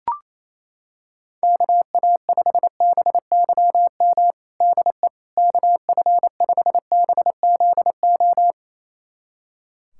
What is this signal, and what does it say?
Morse code, heard as a keyed tone, about 27 wpm, 700 Hz. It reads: KA5BYM DE KF5BZO
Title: USA Callsign DE Callsign 28wpm